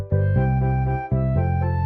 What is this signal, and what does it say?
There is no code.